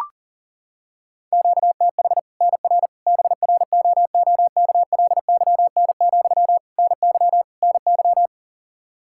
Morse code, heard as a keyed tone, about 37 wpm, 700 Hz. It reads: QTH NR BROOKLYN, NY NY
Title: Practice QSO - Exchange 1 - Beginning - 40wpm